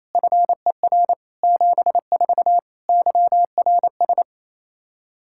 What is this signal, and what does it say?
Morse code, heard as a keyed tone, about 28 wpm, 700 Hz. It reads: FER 74 YRS
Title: K1USN SST QSO Practice 28wpm